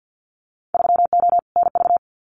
Morse code, heard as a keyed tone, about 50 wpm, 700 Hz. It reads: 3O NV
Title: Winter Field Day Exchanges Encore 50wpm